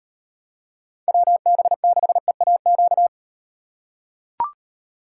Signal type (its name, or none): Morse code